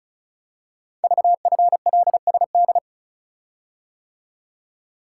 Morse code, heard as a keyed tone, about 36 wpm, 690 Hz. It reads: VFLSD
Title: Code Groups - 5 Letters - 35wpm